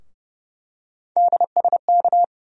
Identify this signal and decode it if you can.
Morse code, heard as a keyed tone, about 30 wpm, 700 Hz. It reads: DSK
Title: USA Call Sign Postfixes - 3 Letters 30wpm